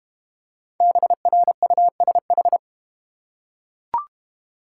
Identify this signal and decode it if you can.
Morse code, heard as a keyed tone, about 31 wpm, 700 Hz. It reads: BRUSH
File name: Easy Words - Speed-Racing with Hints - 25wpm